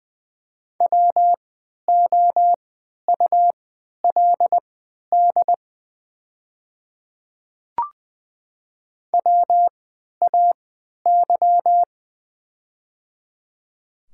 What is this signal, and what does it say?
Morse code, heard as a keyed tone, about 20 wpm, spaced as for 11 wpm, 690 Hz. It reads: WOULD WAY
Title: 3x Character Spacing - Top 100 Words - 20wpm